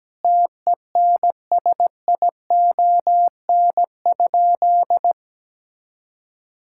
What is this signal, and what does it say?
Morse code, heard as a keyed tone, about 17 wpm, 700 Hz. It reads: TENSION?
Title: Sentences from Top 1600 Words 17wpm